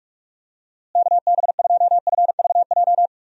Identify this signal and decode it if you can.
Morse code, heard as a keyed tone, about 46 wpm, 700 Hz. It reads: KB2FVJ
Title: USA Call Signs - 6 Characters - Repeated 2x - 45wpm